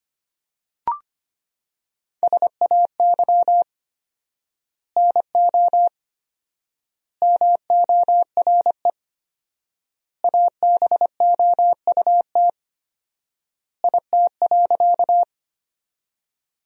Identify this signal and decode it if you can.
Morse code, heard as a keyed tone, about 25 wpm, 700 Hz. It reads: SAY NO MORE ABOUT IT.
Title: 4x Word-Spacing - Sentences from Top 200 Words - 25wpm